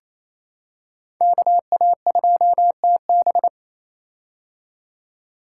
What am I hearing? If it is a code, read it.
Morse code, heard as a keyed tone, about 28 wpm, 700 Hz. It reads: KA2TB
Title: Code Groups - 5 Letter:Numbers - 28wpm